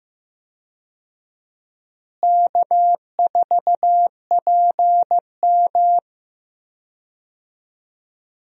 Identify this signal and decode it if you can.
Morse code, heard as a keyed tone, about 15 wpm, 700 Hz. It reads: K4PM
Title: USA Call Signs - 4 Characters - Repeated 3x - 15wpm